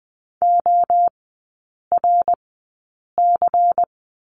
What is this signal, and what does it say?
Morse code, heard as a keyed tone, about 20 wpm, spaced as for 8 wpm, 700 Hz. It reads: ORC
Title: 107 Lesson 18e - Letter m - Three Characters 20wpm